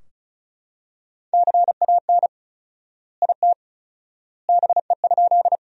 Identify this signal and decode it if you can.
Morse code, heard as a keyed tone, about 36 wpm, 700 Hz. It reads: CAN IT BE?
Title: 4x Word-Spacing - Sentences from Top 100 Words - 35wpm